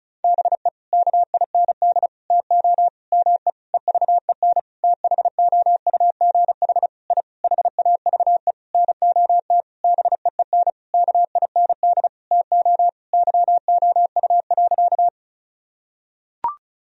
Morse code, heard as a keyed tone, about 34 wpm, 700 Hz. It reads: BE KIND TO ME EVEN THOUGH I HAVE NOT BEEN KIND TO YOU.
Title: Sentences from Top 1000 Words - Review Part 2 - 35wpm